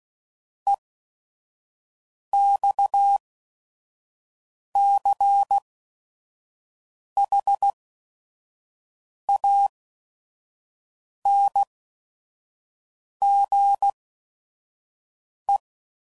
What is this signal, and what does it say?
Morse code, heard as a keyed tone, about 16 wpm, spaced as for 5 wpm, 800 Hz. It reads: EXCHANGE
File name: CW Code Groups @5WPM / QSOs / QSO WA7QOX 1